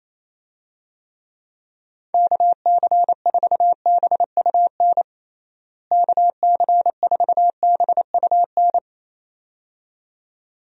Morse code, heard as a keyed tone, about 28 wpm, 700 Hz. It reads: KC4BUN KC4BUN
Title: USA Call Signs - 6 Characters - Repeated 2x - 28wpm